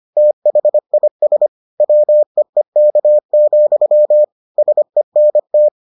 Morse code, heard as a keyed tone, about 25 wpm, 600 Hz. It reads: THIS WEEK, SENT